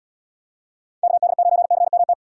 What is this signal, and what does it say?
Morse code, heard as a keyed tone, about 78 wpm, 700 Hz. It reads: KD9CMT